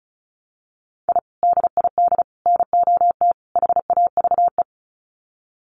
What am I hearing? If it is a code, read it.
Morse code, heard as a keyed tone, about 34 wpm, 700 Hz. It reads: I DID NOT HAVE
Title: Sets of 4 Words 35wpm